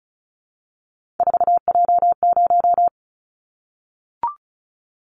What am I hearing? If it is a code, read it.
Morse code, heard as a keyed tone, about 34 wpm, 700 Hz. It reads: 4J0
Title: Callsign Prefixes for Top 100 DXCC - 35wpm